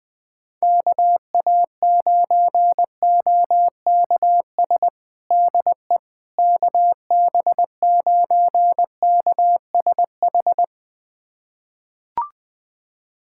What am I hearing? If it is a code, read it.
Morse code, heard as a keyed tone, about 21 wpm, 700 Hz. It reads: KA9OKS DE KB9KSH
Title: USA Callsign DE Callsign - 6 Characters - 20wpm